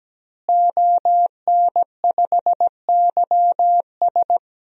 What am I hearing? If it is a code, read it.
Morse code, heard as a keyed tone, about 17 wpm, 700 Hz. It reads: ON5YS
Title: DXCC-100 Call Signs - 5 Characters - Repeated 3x - 17wpm